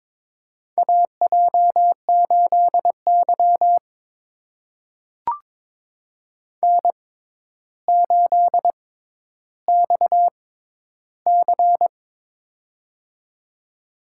Morse code, heard as a keyed tone, about 22 wpm, spaced as for 11 wpm, 700 Hz. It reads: AJ8Y N8XC
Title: USA Call Signs - 6x Character Spacing - 22wpm